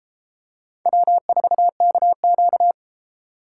Morse code, heard as a keyed tone, about 33 wpm, 700 Hz. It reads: W4KQ